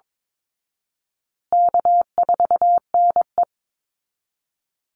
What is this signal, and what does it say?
Morse code, heard as a keyed tone, about 22 wpm, 700 Hz. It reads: K4NE
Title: Easy Call Signs Repeated 3x - 22wpm